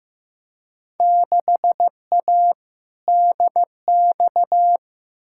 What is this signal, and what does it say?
Morse code, heard as a keyed tone, about 15 wpm, 700 Hz. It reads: 6A DX
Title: Field Day Exchanges Encore - 15wpm